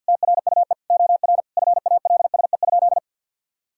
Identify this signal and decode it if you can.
Morse code, heard as a keyed tone, about 50 wpm, 700 Hz. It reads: TRUE OR FALSE?
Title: Sentences from Top 1600 Words 50wpm